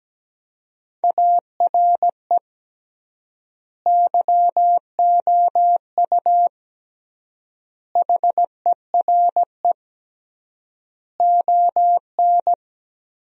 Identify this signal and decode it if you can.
Morse code, heard as a keyed tone, about 17 wpm, 700 Hz. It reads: ARE YOU HERE ON
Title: 3x Word-Spacing - Sentences from Top 400 Words - 17wpm